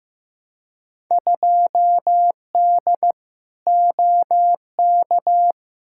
Morse code, heard as a keyed tone, about 15 wpm, 700 Hz. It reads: 2D OK
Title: Field Day Exchanges Encore - 15wpm